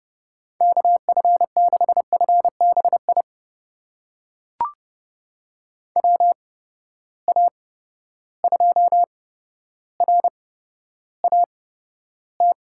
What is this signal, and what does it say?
Morse code, heard as a keyed tone, about 30 wpm, spaced as for 13 wpm, 700 Hz. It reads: KF6FBI WA2RAT
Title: USA Call Signs - 6 Characters - 8x Character Spacing - 30wpm